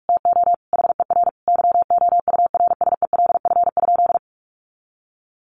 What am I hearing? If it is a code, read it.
Morse code, heard as a keyed tone, about 44 wpm, 700 Hz. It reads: TO HER YOURSELF?
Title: Sentences from Top 1500 Words 45wpm